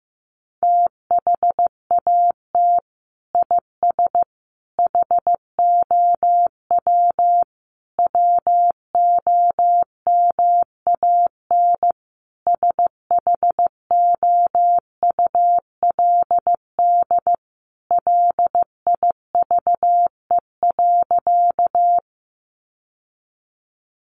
Morse code, heard as a keyed tone, about 15 wpm, 700 Hz. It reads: THAT IS HOW WOMAN SHOULD LIVE.
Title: Sentences from Top 1100 Words 15wpm